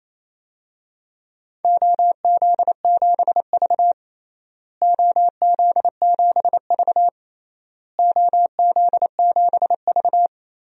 Morse code, heard as a keyed tone, about 28 wpm, 700 Hz. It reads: OZ7V OZ7V OZ7V
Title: DXCC-100 Call Signs - 4 Characters - Repeated 3x - 28wpm